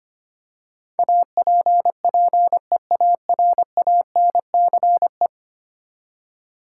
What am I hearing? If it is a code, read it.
Morse code, heard as a keyed tone, about 25 wpm, 700 Hz. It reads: APPEARANCE
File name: Ten Letter Words - 25wpm